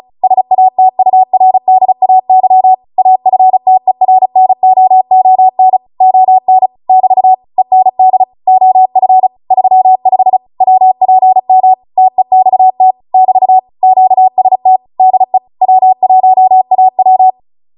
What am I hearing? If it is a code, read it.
Morse code, heard as a keyed tone, about 36 wpm, 750 Hz. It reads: SATURDAY AFTERNOON ON <BT> END OF 35 WPM TEXT <BT> QST DE W1AW
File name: Morse / 35WPM